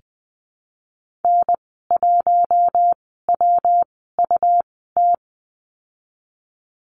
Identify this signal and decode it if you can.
Morse code, heard as a keyed tone, about 20 wpm, spaced as for 15 wpm, 700 Hz. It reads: N1WUT
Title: 122 Lesson 20g - Number 3 - Call Signs 20wpm